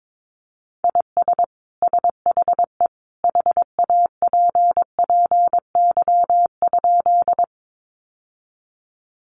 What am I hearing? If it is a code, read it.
Morse code, heard as a keyed tone, about 22 wpm, 700 Hz. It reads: IS SHE HAPPY?
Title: Sentences from Top 1000 Words 22wpm